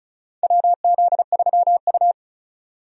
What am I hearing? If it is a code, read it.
Morse code, heard as a keyed tone, about 36 wpm, 700 Hz. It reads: WZ3U
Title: USA Call Signs - 4 Characters - 35wpm